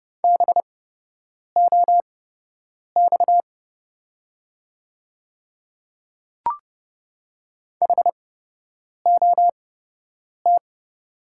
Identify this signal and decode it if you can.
Morse code, heard as a keyed tone, about 30 wpm, spaced as for 8 wpm, 700 Hz. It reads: BOX HOT
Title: 8x Character Spacing - Top 500 Words - 30wpm